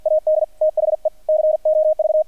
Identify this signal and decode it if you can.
Morse code, heard as a keyed tone, about 50 wpm, 620 Hz, with radio noise. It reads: AD THE COV
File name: archived / CRSnet